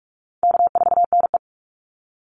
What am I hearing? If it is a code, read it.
Morse code, heard as a keyed tone, about 44 wpm, 700 Hz. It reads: K4NE
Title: Easy Call Signs - 45wpm